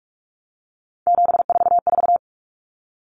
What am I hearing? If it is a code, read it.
Morse code, heard as a keyed tone, about 44 wpm, 700 Hz. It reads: 744
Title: Numbers - 3-Digit 45wpm